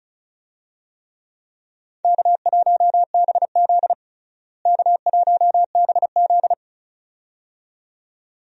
Morse code, heard as a keyed tone, about 36 wpm, 700 Hz. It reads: K1BZ K1BZ
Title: USA Call Signs - 4 Characters - Repeated 2x - 35wpm